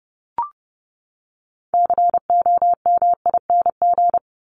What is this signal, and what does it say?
Morse code, heard as a keyed tone, about 30 wpm, 700 Hz. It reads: COMING
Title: ING Suffix - Top 100 Words with Base Words 30wpm